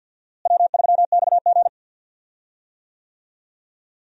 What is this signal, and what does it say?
Morse code, heard as a keyed tone, about 46 wpm, 700 Hz. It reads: W3XG
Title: USA Call Signs - 4 Characters - Repeated 2x - 50wpm